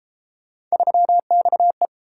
Morse code, heard as a keyed tone, about 30 wpm, 700 Hz. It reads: 3XE